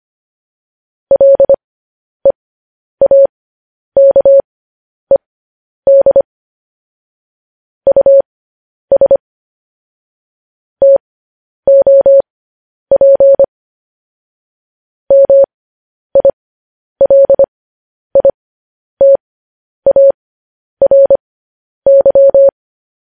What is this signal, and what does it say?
Morse code, heard as a keyed tone, about 25 wpm, spaced as for 10 wpm, 550 Hz. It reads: LEAKED US TOP MILITARY
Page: news-headlines-in-morse-code-at-10-wpm